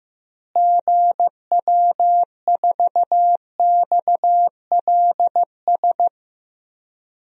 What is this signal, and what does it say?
Morse code, heard as a keyed tone, about 15 wpm, 700 Hz. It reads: GW4XLS
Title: DXCC-100 Call Signs - 6 Characters - Repeated 3x - 15wpm